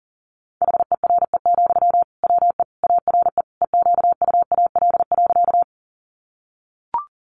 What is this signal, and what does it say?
Morse code, heard as a keyed tone, about 40 wpm, 700 Hz. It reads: HERE, WE ARE EQUAL.
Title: Sentences from Top 1400 Words 40wpm